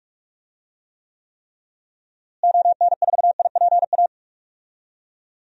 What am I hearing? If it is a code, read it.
Morse code, heard as a keyed tone, about 46 wpm, 700 Hz. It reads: ON4IPA
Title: DXCC-100 Call Signs - 6 Characters - Repeated 3x - 45wpm